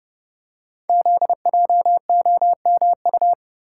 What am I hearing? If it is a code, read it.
Morse code, heard as a keyed tone, about 28 wpm, 700 Hz. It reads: ZJOMU
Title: Code Groups - 5 Letters - 30wpm